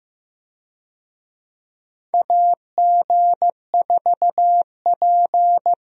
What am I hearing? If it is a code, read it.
Morse code, heard as a keyed tone, about 15 wpm, 700 Hz. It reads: AG4P
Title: USA Call Signs - 4 Characters - Repeated 3x - 15wpm